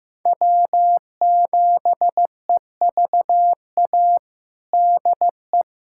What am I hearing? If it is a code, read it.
Morse code, heard as a keyed tone, about 16 wpm, 700 Hz. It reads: W7EVA DE
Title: USA Callsign DE Callsign 15wpm